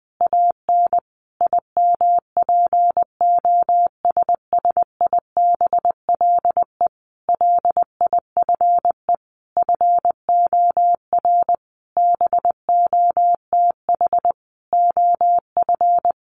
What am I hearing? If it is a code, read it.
Morse code, heard as a keyed tone, about 20 wpm, 700 Hz. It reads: AN IMPOSSIBLE LIFE FOR BOTH OF
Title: Sentences from Top 1100 Words 20wpm